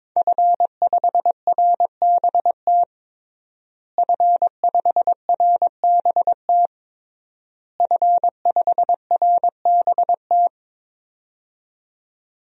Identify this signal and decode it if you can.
Morse code, heard as a keyed tone, about 22 wpm, 700 Hz. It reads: F5RBT F5RBT F5RBT
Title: DXCC-100 Call Signs - 5 Characters - Repeated 3x - 22wpm